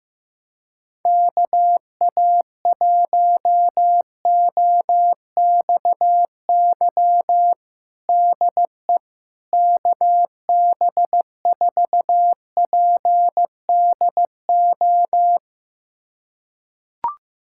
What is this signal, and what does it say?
Morse code, heard as a keyed tone, about 15 wpm, 700 Hz. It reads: KA1OXY DE KB4PDO
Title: USA Callsign DE Callsign - 6 Characters - 15wpm